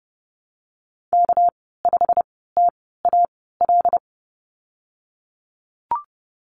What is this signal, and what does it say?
Morse code, heard as a keyed tone, about 30 wpm, spaced as for 17 wpm, 700 Hz. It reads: K5TAL
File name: Lesson 37g - Letter x - Call Signs 30wpm